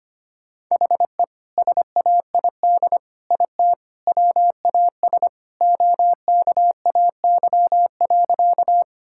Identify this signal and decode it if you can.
Morse code, heard as a keyed tone, about 25 wpm, 700 Hz. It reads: HE SAID IT WAS OKAY.
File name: practice / 25wpm / Sentences with 2-Word Phrases - 25wpm